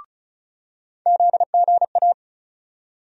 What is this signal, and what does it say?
Morse code, heard as a keyed tone, about 36 wpm, 700 Hz. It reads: ZGA
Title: USA Call Sign Postfixes - 3 Letters 35wpm